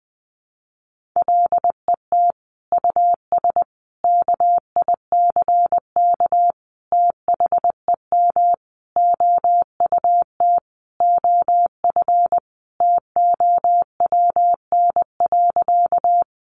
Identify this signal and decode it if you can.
Morse code, heard as a keyed tone, about 20 wpm, 700 Hz. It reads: LET US KICK THEM OUT OF TOWN.
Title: Sentences from Top 1300 Words 20wpm